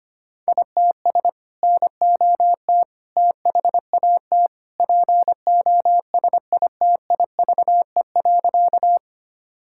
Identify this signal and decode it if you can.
Morse code, heard as a keyed tone, about 25 wpm, 700 Hz. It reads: ITS NOT THAT POSITIVE.
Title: Sentences from Top 1000 Words - Review Part 5 - 25wpm